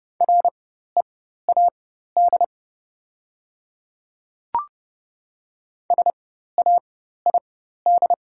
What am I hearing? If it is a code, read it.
Morse code, heard as a keyed tone, about 30 wpm, spaced as for 14 wpm, 700 Hz. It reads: READ SAID
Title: 4x Character Spacing - Top 200 Words - 30wpm